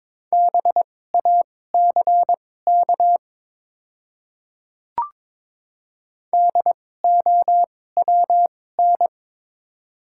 Morse code, heard as a keyed tone, about 22 wpm, spaced as for 16 wpm, 700 Hz. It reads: BACK DOWN
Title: 2x Character Spacing - Top 200 Words - 22wpm